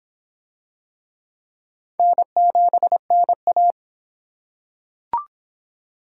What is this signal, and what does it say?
Morse code, heard as a keyed tone, about 22 wpm, 700 Hz. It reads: N7NA